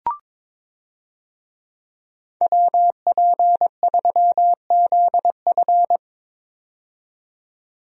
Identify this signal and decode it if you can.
Morse code, heard as a keyed tone, about 22 wpm, 700 Hz. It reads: WP3ZF
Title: USA Call Signs - 5 Characters - 22wpm